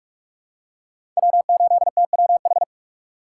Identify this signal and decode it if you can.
Morse code, heard as a keyed tone, about 41 wpm, 700 Hz. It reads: W8TWH
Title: USA Call Signs - 2x Character Spacing - 45wpm